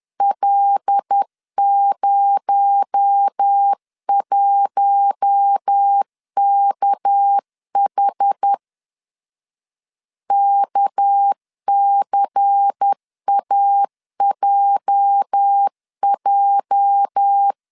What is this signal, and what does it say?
Morse code, heard as a keyed tone, about 10 wpm, 800 Hz. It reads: L01KH KCAJJ
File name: LEZIONI / 8 YJ45 / 8 YJ 10wpm 2651